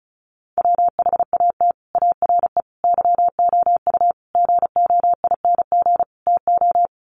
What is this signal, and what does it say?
Morse code, heard as a keyed tone, about 34 wpm, 700 Hz. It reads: WHAT ARE YOU GOING TO
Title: Sets of 5 Words 35wpm